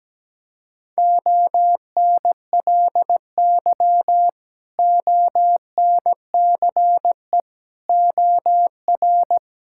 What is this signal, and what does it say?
Morse code, heard as a keyed tone, about 17 wpm, 700 Hz. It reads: ONLY ONCE OR
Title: Sentences from Top 1600 Words 17wpm